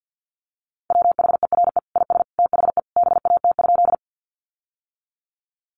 Morse code, heard as a keyed tone, about 50 wpm, 700 Hz. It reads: WHERE IS THE BAT?
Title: Sentences from Top 1500 Words 50wpm